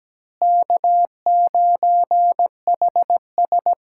Morse code, heard as a keyed tone, about 17 wpm, 700 Hz. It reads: K9HS
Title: USA Callsign DE Callsign 17wpm